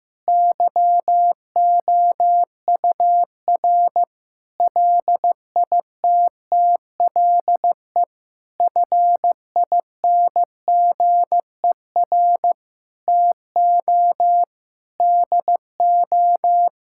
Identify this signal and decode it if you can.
Morse code, heard as a keyed tone, about 15 wpm, 700 Hz. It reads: YOUR LITTLE FINGER TO DO
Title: Sentences from Top 1600 Words 15wpm